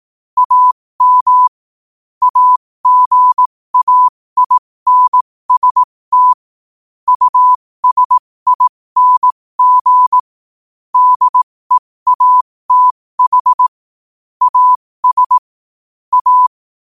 Morse code, heard as a keyed tone, about 18 wpm, spaced as for 15 wpm, 1000 Hz. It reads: AM AGAINST USING DEATH AS A
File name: qotdfiles / 15wpm